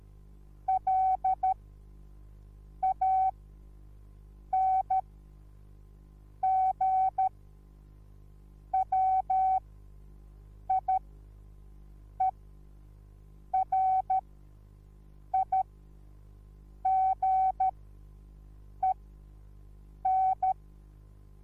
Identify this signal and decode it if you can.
Morse code, heard as a keyed tone, about 13 wpm, spaced as for 6 wpm, 760 Hz. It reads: LANGWIERIGEN